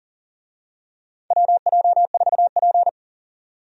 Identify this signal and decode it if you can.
Morse code, heard as a keyed tone, about 40 wpm, 700 Hz. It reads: WJ4P
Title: USA Call Signs - 4 Characters - 40wpm